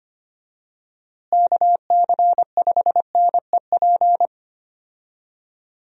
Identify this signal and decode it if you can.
Morse code, heard as a keyed tone, about 25 wpm, 700 Hz. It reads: KC5NEP
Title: USA Call Signs - 6 Characters - Repeated 2x - 25wpm